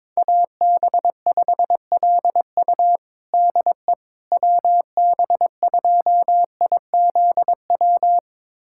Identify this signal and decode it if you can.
Morse code, heard as a keyed tone, about 22 wpm, 700 Hz. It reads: AB5LU DE WB2IZW
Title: USA Callsign DE Callsign 22wpm